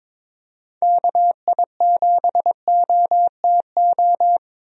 Morse code, heard as a keyed tone, about 22 wpm, 700 Hz. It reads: KI7OTO
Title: USA Call Signs - 6 Characters - 22wpm